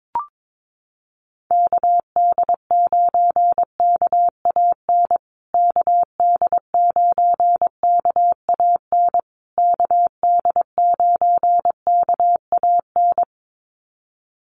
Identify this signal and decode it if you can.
Morse code, heard as a keyed tone, about 22 wpm, 700 Hz. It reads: KD9KAN KD9KAN KD9KAN
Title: USA Call Signs Repeated 3x 22wpm